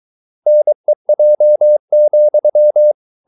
Morse code, heard as a keyed tone, about 23 wpm, 600 Hz. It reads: NEJ,
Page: Ferie 23 wpm